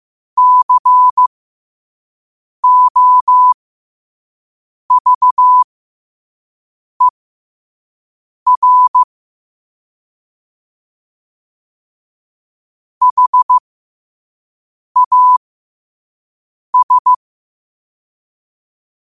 Morse code, heard as a keyed tone, about 15 wpm, spaced as for 5 wpm, 1000 Hz. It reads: COVER HAS
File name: qotdfiles / 5wpm